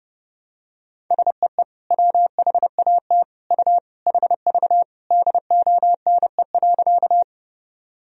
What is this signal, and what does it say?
Morse code, heard as a keyed tone, about 32 wpm, 700 Hz. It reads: SEE WHAT U HV DONE.
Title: Sentences from Top 500 Words and Abbreviations 30wpm